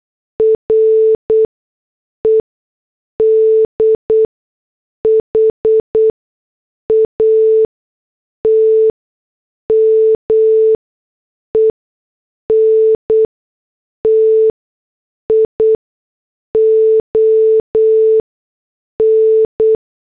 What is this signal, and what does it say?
Morse code, heard as a keyed tone, about 8 wpm, spaced as for 6 wpm, 440 Hz. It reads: RED HAT MENTION
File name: morse / headlines